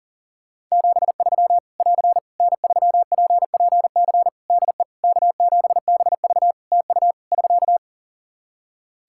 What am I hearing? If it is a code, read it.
Morse code, heard as a keyed tone, about 37 wpm, 700 Hz. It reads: 73 <AR> N3PPC DE K7BV TU <SK>
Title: Practice QSO - Exchange 4 - Ending - 40wpm